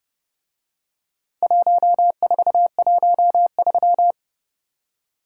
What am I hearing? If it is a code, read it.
Morse code, heard as a keyed tone, about 28 wpm, 700 Hz. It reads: 1413